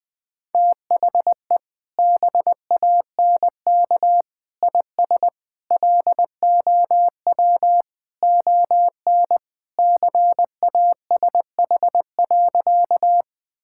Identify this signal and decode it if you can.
Morse code, heard as a keyed tone, about 21 wpm, 700 Hz. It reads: THE BANK IS LOW ON CASH.
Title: Sentences from Top 1000 Words - Review Part 5 - 20wpm